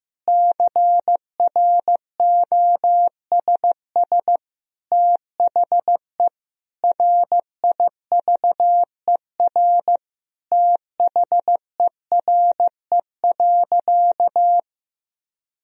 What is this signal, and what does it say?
Morse code, heard as a keyed tone, about 15 wpm, 700 Hz. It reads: CROSS THE RIVER THERE.
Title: Sentences from Top 1000 Words - Review Part 3 - 15wpm